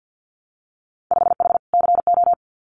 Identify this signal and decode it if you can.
Morse code, heard as a keyed tone, about 50 wpm, 700 Hz. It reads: 5H CO
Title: Winter Field Day Exchanges Encore 50wpm